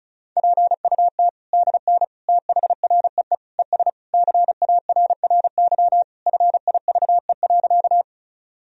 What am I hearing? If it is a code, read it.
Morse code, heard as a keyed tone, about 36 wpm, 700 Hz. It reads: PUT DN THREE ES CARRY FIVE.
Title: Sentences from Top 400 Words and Abbreviations 35wpm